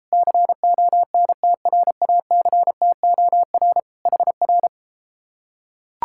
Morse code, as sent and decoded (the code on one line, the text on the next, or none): -.-. --- -. - .-. .- -.-. - --- .-. / .... .-.
CONTRACTOR HR